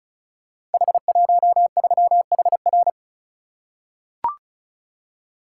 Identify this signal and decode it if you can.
Morse code, heard as a keyed tone, about 36 wpm, 700 Hz. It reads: H13HR
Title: Code Groups - 5 Letter:Numbers - 35wpm